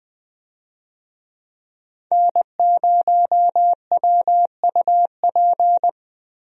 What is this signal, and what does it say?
Morse code, heard as a keyed tone, about 20 wpm, 700 Hz. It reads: N0WUP
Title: USA Call Signs - 5 Characters - Repeated 2x - 20wpm